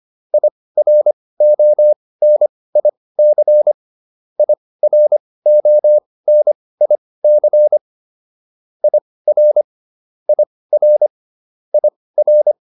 Morse code, heard as a keyed tone, about 25 wpm, spaced as for 18 wpm, 600 Hz. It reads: IRONIC IRONIC IR IR IR